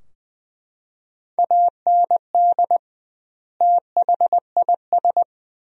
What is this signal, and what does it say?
Morse code, heard as a keyed tone, about 20 wpm, 700 Hz. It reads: AND THIS
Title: 2x Word-Spacing - Sentences from Top 300 Words - 20wpm